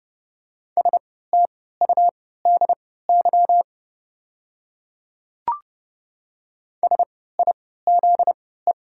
Morse code, heard as a keyed tone, about 30 wpm, spaced as for 17 wpm, 700 Hz. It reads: STUDY SIZE
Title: 3x Character Spacing - Top 500 Words - 30wpm